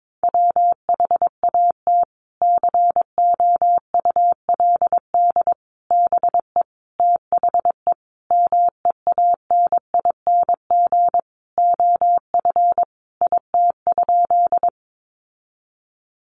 Morse code, heard as a keyed tone, about 22 wpm, 700 Hz. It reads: WHAT COULD BE THE MEANING OF IT?
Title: Sentences from Top 900 Words 22wpm